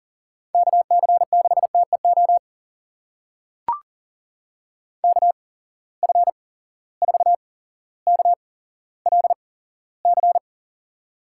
Morse code, heard as a keyed tone, about 40 wpm, spaced as for 17 wpm, 700 Hz. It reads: KC6TEO KF4KLC
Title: USA Call Signs - 6 Characters - 8x Character Spacing - 40wpm